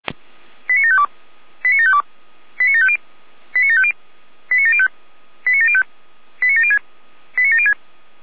Five-tone selective call (ZVEI), heard as quick runs of five tones, tones of about 70 ms. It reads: 89752 89752 89750 89750 89705 89705 89706 89706